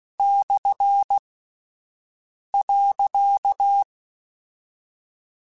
Morse code, heard as a keyed tone, about 16 wpm, 800 Hz. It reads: / .
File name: Misc / Misc-mp3s / Punct @10